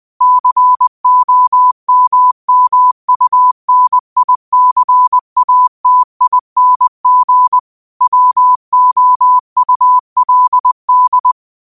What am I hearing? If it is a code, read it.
Morse code, heard as a keyed tone, about 20 wpm, 1000 Hz. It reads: COMMUNICATING WOULD
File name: 20wpm